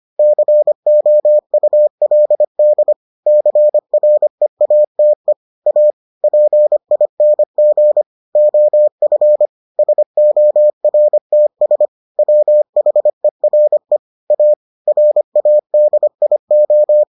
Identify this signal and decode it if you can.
Morse code, heard as a keyed tone, about 25 wpm, 600 Hz. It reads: COULD CREATE A PING OF SORTS WHERE A RADIO